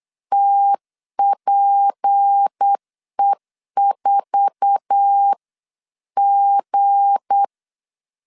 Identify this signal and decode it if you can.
Morse code, heard as a keyed tone, about 8 wpm, 800 Hz. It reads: TPE4G
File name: CW / DET / 1 9 1500C S B 8wpm 5624